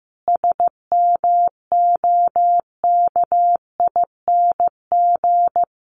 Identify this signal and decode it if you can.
Morse code, heard as a keyed tone, about 15 wpm, 700 Hz. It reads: SMOKING
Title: A Sound Of Thunder 15wpm